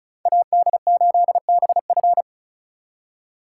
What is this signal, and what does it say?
Morse code, heard as a keyed tone, about 44 wpm, 700 Hz. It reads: AD8BF